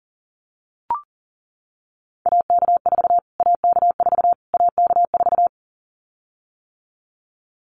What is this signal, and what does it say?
Morse code, heard as a keyed tone, about 40 wpm, 700 Hz. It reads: AK4 AK4 AK4
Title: Callsign Prefixes for Top 25 DXCC - Repeated 3x - 40wpm